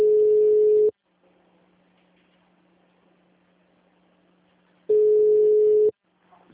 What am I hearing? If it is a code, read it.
Ringback tone (Europe, 425 Hz), with speech.
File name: out- -2002- - - .27175